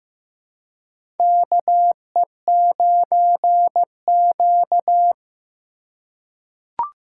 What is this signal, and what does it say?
Morse code, heard as a keyed tone, about 15 wpm, 700 Hz. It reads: KE9Q
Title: USA Call Signs - 4 Characters - 3x Character Spacing - 15wpm